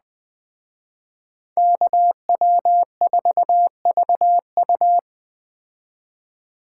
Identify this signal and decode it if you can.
Morse code, heard as a keyed tone, about 20 wpm, 700 Hz. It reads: KW4VU
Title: Speed Racing - USA 2x2 Call Signs 17wpm